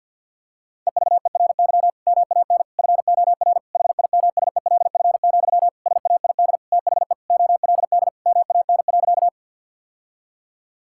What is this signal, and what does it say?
Morse code, heard as a keyed tone, about 50 wpm, 700 Hz. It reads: EVERY MAN FOR HIMSELF, SAID THE OLD MAN.
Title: Sentences from Top 1600 Words 50wpm